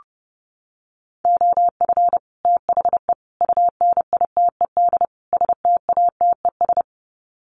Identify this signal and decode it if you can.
Morse code, heard as a keyed tone, about 30 wpm, 700 Hz. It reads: OF THE UNITED STATES
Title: Sets of 4 Words 30wpm